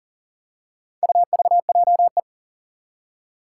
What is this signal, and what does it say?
Morse code, heard as a keyed tone, about 39 wpm, 700 Hz. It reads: UVJE
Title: Four Challenging Letter-Number Combinations 40wpm